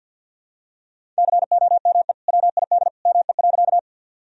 Morse code, heard as a keyed tone, about 50 wpm, 700 Hz. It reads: COME WID ME.
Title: Sentences from Top 200 Words and Abbreviations 50wpm